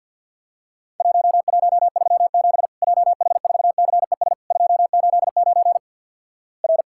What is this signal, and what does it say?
Morse code, heard as a keyed tone, about 50 wpm, 700 Hz. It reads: 1137 JH4CES 289 R
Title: Worked All Europe DX Contest - QTC Exchange - 50wpm